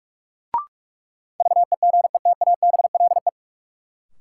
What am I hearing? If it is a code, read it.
Morse code, heard as a keyed tone, about 42 wpm, 700 Hz. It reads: VEGETABLE
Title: Sentences from Top 1300 Words - New Word Drill Alt - 45wpm